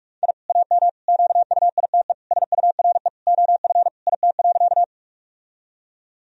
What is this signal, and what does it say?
Morse code, heard as a keyed tone, about 46 wpm, 700 Hz. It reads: I AM QUITE SURE OF IT.
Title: Sentences from Top 1000 Words - Review Part 2 - 45wpm